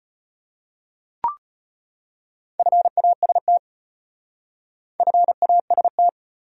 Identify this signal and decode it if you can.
Morse code, heard as a keyed tone, about 34 wpm, 700 Hz. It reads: FAST FAST